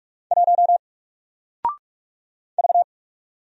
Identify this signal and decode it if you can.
Morse code, heard as a keyed tone, about 44 wpm, 700 Hz. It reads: 1 V
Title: Single Letter-Number 45wpm